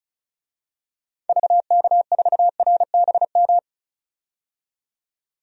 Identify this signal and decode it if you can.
Morse code, heard as a keyed tone, about 34 wpm, 700 Hz. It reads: VK4RBM
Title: DXCC-100 Call Signs - 6 Characters - 35wpm